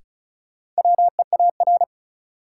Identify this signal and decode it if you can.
Morse code, heard as a keyed tone, about 34 wpm, 700 Hz. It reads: WEAR
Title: Sentences from Top 1500 Words - New Word Drill Alt - 35wpm